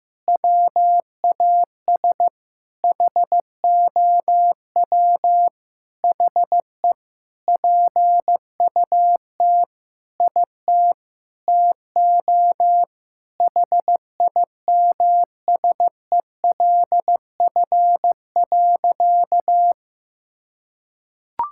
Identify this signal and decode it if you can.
Morse code, heard as a keyed tone, about 15 wpm, 700 Hz. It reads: WAS HOW HE PUT IT TO HIMSELF.
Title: Sentences from Top 1600 Words 15wpm